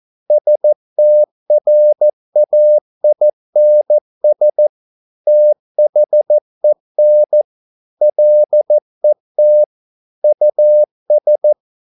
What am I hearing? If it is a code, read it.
Morse code, heard as a keyed tone, about 14 wpm, 600 Hz. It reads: STRAINS THEN LET US